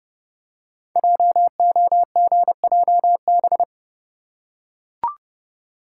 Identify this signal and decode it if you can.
Morse code, heard as a keyed tone, about 30 wpm, 700 Hz. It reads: JOGJB
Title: Code Groups - 5 Letters - 30wpm